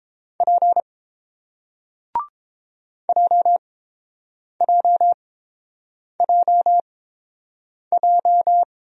Morse code, heard as a keyed tone, about 31 wpm, spaced as for 7 wpm, 690 Hz. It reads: P JJJJ